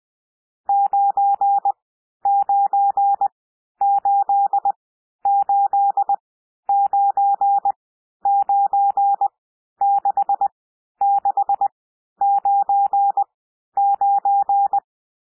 Morse code, heard as a keyed tone, about 20 wpm, spaced as for 11 wpm, 800 Hz. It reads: 9988996699
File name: cwmp3 / N1 SOUND NUMBERS 20 wpm